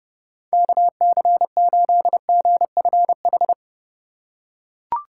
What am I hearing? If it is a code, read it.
Morse code, heard as a keyed tone, about 28 wpm, 700 Hz. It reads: KC8GFH